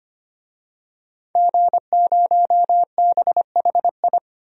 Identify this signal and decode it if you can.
Morse code, heard as a keyed tone, about 25 wpm, 700 Hz. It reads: G0BHI
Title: DXCC-100 Call Signs - 5 Characters - 25wpm